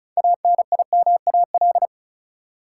Morse code, heard as a keyed tone, about 36 wpm, 700 Hz. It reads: ANIMAL
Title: Sentences from Top 1000 Words - Review Part 1 - New Word Drill - 35wpm